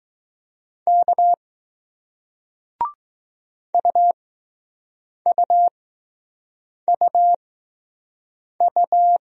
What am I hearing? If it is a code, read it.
Morse code, heard as a keyed tone, about 20 wpm, spaced as for 6 wpm, 700 Hz. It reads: K UUUU